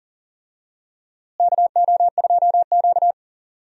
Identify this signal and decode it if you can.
Morse code, heard as a keyed tone, about 40 wpm, 700 Hz. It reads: KO2Q
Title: USA Call Signs - 4 Characters - Repeated 2x - 40wpm